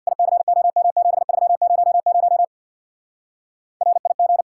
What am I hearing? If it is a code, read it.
Morse code, heard as a keyed tone, about 62 wpm, 700 Hz. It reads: ICOM7300 RIG
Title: Speed-Racing Common QSO Phrases - 50wpm